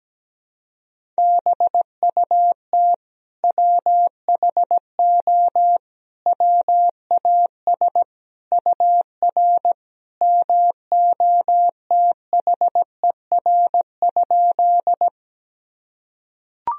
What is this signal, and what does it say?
Morse code, heard as a keyed tone, about 17 wpm, 700 Hz. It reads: BUT WHO WAS UR MOTHER?
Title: Sentences from Top 200 Words and Abbreviations 17wpm